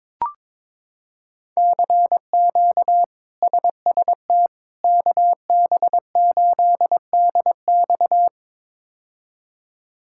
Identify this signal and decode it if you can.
Morse code, heard as a keyed tone, about 22 wpm, 700 Hz. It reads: CQ SST KB8DX
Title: K1USN SST QSO Practice - 22wpm